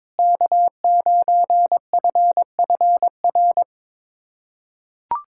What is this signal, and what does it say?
Morse code, heard as a keyed tone, about 22 wpm, 700 Hz. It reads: K9FFR
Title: USA Callsign DE Callsign - 5 Characters - 22wpm